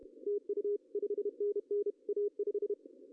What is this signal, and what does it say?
Morse code, heard as a keyed tone, about 30 wpm, 410 Hz. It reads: TU 5NN A5
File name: CW / cqww-cw-k5tr